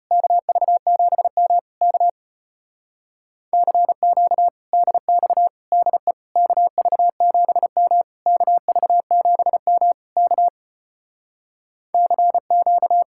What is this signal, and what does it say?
Morse code, heard as a keyed tone, about 33 wpm, 700 Hz. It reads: KV7M K CQ DX DE KV7M KV7M K CQ